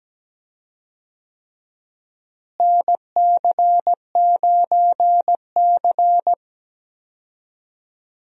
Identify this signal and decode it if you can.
Morse code, heard as a keyed tone, about 17 wpm, 700 Hz. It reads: NC9C
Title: USA Call Signs - 4 Characters - Repeated 3x - 17wpm